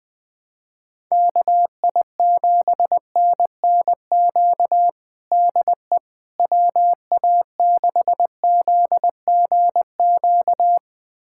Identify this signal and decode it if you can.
Morse code, heard as a keyed tone, about 20 wpm, 700 Hz. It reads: KI7NNQ DE WA6ZGQ
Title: USA Callsign DE Callsign - 6 Characters - 20wpm